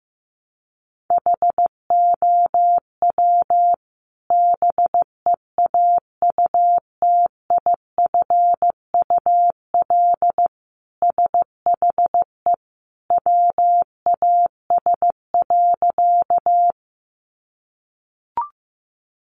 Morse code, heard as a keyed tone, about 15 wpm, 700 Hz. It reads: HOW BEAUTIFUL SHE WAS.
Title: Sentences from Top 900 Words 15wpm